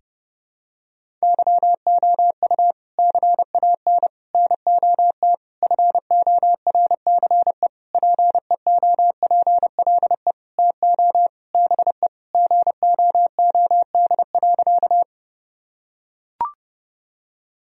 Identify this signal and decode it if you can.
Morse code, heard as a keyed tone, about 32 wpm, 700 Hz. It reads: YOU CAN NOT FORCE PEOPLE TO BE GOOD.
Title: Sentences from Top 1000 Words - Review Part 3 - 30wpm